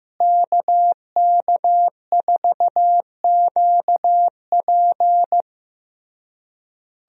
Morse code, heard as a keyed tone, about 15 wpm, 700 Hz. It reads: KK4QP
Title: USA Call Signs - 5 Characters - Repeated 3x - 15wpm